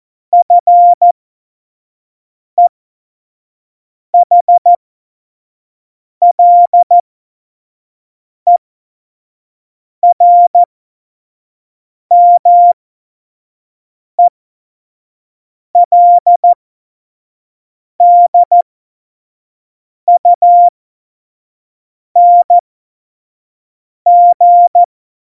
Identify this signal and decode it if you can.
Morse code, heard as a keyed tone, about 14 wpm, spaced as for 5 wpm, 700 Hz. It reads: FEHLERMELDUNG
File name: Morsekurs / Witze / Witz 29. Januar 2019, 25 F70